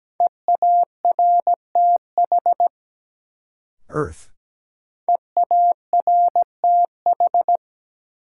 Morse code, heard as a keyed tone, about 17 wpm, 700 Hz. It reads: EARTH EARTH
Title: Sets of 5 Words Encore - New Word Drill - 17wpm